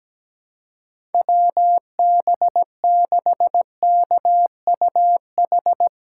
Morse code, heard as a keyed tone, about 17 wpm, 700 Hz. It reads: WB6KUH
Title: USA Callsign DE Callsign - 6 Characters - 17wpm